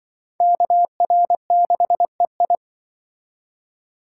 Morse code, heard as a keyed tone, about 24 wpm, 700 Hz. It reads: KR6EI
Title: Speed Racing - USA Call Signs 20wpm